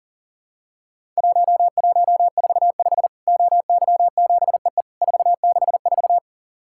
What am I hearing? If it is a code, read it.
Morse code, heard as a keyed tone, about 40 wpm, 700 Hz. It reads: 1145 OY7EE 464
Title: Worked All Europe DX Contest - QTC Exchange - 40wpm